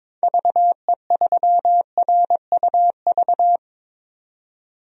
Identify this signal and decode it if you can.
Morse code, heard as a keyed tone, about 22 wpm, 700 Hz. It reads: VE3RUV
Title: DXCC-100 Call Signs - 6 Characters - 22wpm